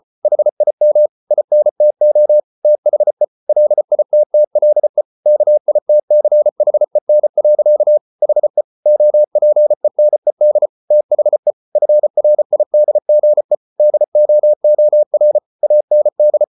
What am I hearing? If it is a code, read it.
Morse code, heard as a keyed tone, about 34 wpm, 600 Hz. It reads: HIM INTO THE LITTLE KITCHEN. HE OPENED THE FRIDGE DOOR AND